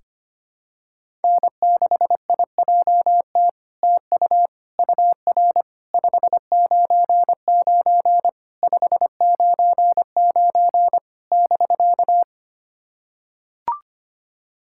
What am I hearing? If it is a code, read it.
Morse code, heard as a keyed tone, about 25 wpm, 700 Hz. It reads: N6IJT TU UR 599 599 <BK>
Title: SOTA QSO Practice 25wpm